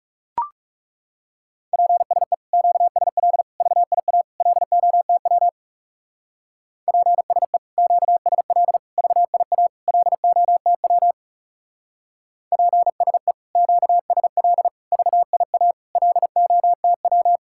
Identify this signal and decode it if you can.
Morse code, heard as a keyed tone, about 36 wpm, 700 Hz. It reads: PSE QSL VIA LOTW PSE QSL VIA LOTW PSE QSL VIA LOTW
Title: Speed-Racing Common QSO Phrases - 30wpm